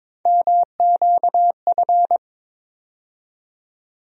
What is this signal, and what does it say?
Morse code, heard as a keyed tone, about 22 wpm, 700 Hz. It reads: MQF
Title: USA Call Sign Postfixes - 3 Letters 22wpm